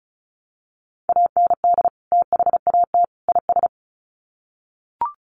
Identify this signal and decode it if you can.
Morse code, heard as a keyed tone, about 34 wpm, 700 Hz. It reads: AND THAT IS
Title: Sets of 3 Words 35wpm